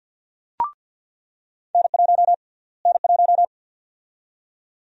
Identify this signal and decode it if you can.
Morse code, heard as a keyed tone, about 46 wpm, 700 Hz. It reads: N1 N1
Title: USA Call Sign Prefixes - Repeated 2x - 50wpm